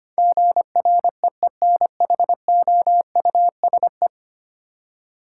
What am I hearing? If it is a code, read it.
Morse code, heard as a keyed tone, about 25 wpm, 700 Hz. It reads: GREENHOUSE
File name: Ten Letter Words - 25wpm